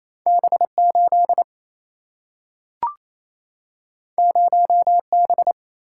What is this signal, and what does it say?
Morse code, heard as a keyed tone, about 28 wpm, 700 Hz. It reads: B8 0B
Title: Single-Letter Single-Number 28wpm